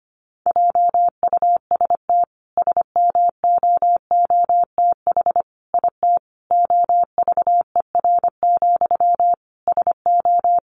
Morse code, heard as a keyed tone, about 25 wpm, 700 Hz. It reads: JUST SMOOTH IT OVER, SO
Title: Sentences from Top 1400 Words 25wpm